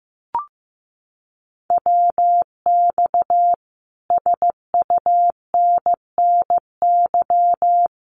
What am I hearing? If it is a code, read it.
Morse code, heard as a keyed tone, about 15 wpm, 700 Hz. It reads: WX SUNNY
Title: Common QSO Phrases Encore 15wpm